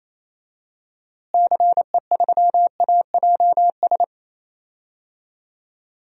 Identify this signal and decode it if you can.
Morse code, heard as a keyed tone, about 28 wpm, 700 Hz. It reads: CE3AJS
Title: DXCC-100 Call Signs - 6 Characters - Repeated 3x - 28wpm